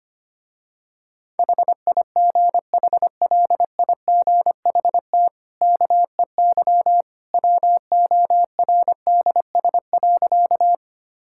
Morse code, heard as a keyed tone, about 25 wpm, 700 Hz. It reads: HIGHLIGHT KEY WORDS.
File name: Easy Sentences with Hints - 25wpm